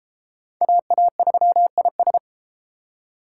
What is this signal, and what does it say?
Morse code, heard as a keyed tone, about 30 wpm, 700 Hz. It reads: AA3IS